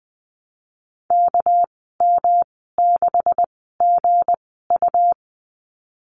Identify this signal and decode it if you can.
Morse code, heard as a keyed tone, about 20 wpm, spaced as for 14 wpm, 700 Hz. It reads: KM6GU
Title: 231 Lesson 37g - Letter x - Call Signs 20wpm